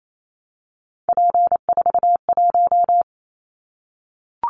Morse code, heard as a keyed tone, about 28 wpm, 700 Hz. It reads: P41
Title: Callsign Prefixes for Top 100 DXCC - 28wpm